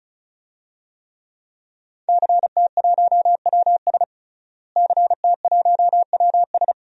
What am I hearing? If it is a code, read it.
Morse code, heard as a keyed tone, about 36 wpm, 700 Hz. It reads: CT1WS CT1WS
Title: DXCC-100 Call Signs - 5 Characters - Repeated 2x - 35wpm